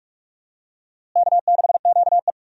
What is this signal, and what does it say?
Morse code, heard as a keyed tone, about 46 wpm, 700 Hz. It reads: K6QE
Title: USA Call Signs - 4 Characters - Repeated 3x - 45wpm